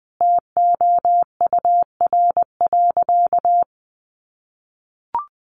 Morse code, heard as a keyed tone, about 20 wpm, 700 Hz. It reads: TOUR.
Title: Sentences from Top 1400 Words 20wpm